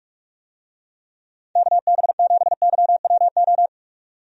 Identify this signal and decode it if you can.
Morse code, heard as a keyed tone, about 42 wpm, 700 Hz. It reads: KB7YWO
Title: USA Call Signs - 6 Characters - 45wpm